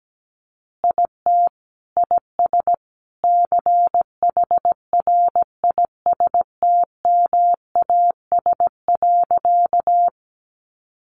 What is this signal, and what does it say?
Morse code, heard as a keyed tone, about 17 wpm, 700 Hz. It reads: IT IS CHRISTMAS.
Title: Easy Phrases 17wpm